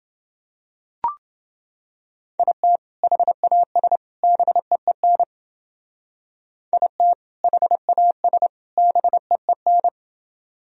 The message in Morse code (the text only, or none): IT HAS BEEN IT HAS BEEN